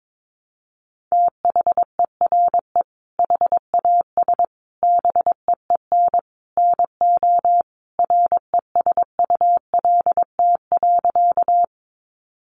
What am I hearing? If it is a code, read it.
Morse code, heard as a keyed tone, about 22 wpm, 700 Hz. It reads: THERE HAS BEEN NO RESULT.
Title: Sentences from Top 700 Words 22wpm